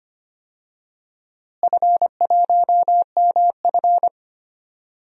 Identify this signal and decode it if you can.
Morse code, heard as a keyed tone, about 25 wpm, 700 Hz. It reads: F1MF
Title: DXCC-100 Call Signs - 4 Characters - Repeated 3x - 25wpm